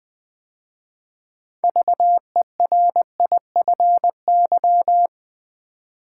Morse code, heard as a keyed tone, about 20 wpm, 700 Hz. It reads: VERIFY
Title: 6 Letter Words - First 3 Letters Given - 20wpm